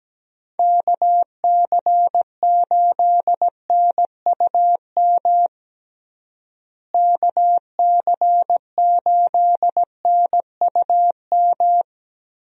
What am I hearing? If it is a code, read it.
Morse code, heard as a keyed tone, about 17 wpm, 700 Hz. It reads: KC8NUM KC8NUM
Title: USA Call Signs - 6 Characters - Repeated 3x - 17wpm